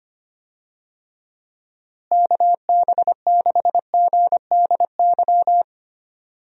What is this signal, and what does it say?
Morse code, heard as a keyed tone, about 25 wpm, 700 Hz. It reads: KB6GDY
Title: USA Call Signs - 6 Characters - Repeated 3x - 25wpm